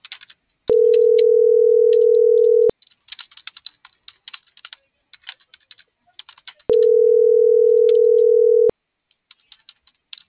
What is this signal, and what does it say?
Ringback tone (North America), 440 + 480 Hz.
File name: CENTRAL MORENA / out ruth.skyangel.24 - .5791